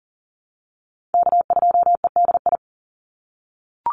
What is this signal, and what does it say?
Morse code, heard as a keyed tone, about 40 wpm, 700 Hz. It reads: K2EDI
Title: USA Call Signs 40wpm